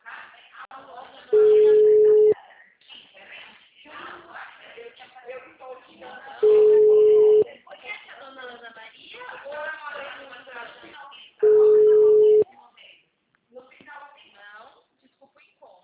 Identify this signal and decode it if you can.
Ringback tone (Europe, 425 Hz), with speech.